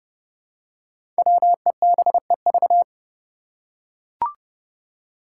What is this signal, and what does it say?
Morse code, heard as a keyed tone, about 30 wpm, 700 Hz. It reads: WEBEV
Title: Five Challenging Letter-Number Combinations 30wpm